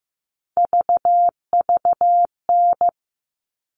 Morse code, heard as a keyed tone, about 15 wpm, 700 Hz. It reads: VVN
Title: All Cut Numbers - 3-Digit 15wpm